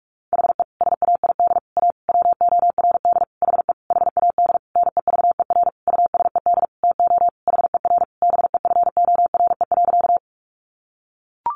Morse code, heard as a keyed tone, about 44 wpm, 700 Hz. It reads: HE SAID A WORD HE HAD NEVER USED TO HER BEFORE.
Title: Sentences from Top 600 Words 45wpm